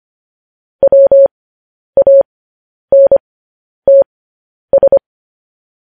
Morse code, heard as a keyed tone, about 25 wpm, spaced as for 10 wpm, 550 Hz. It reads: WANTS
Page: news-headlines-in-morse-code-at-10-wpm